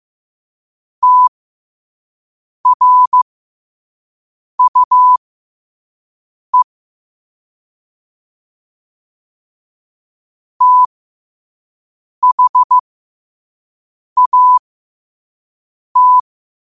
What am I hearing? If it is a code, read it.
Morse code, heard as a keyed tone, about 15 wpm, spaced as for 5 wpm, 1000 Hz. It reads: TRUE THAT
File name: qotdfiles / 5wpm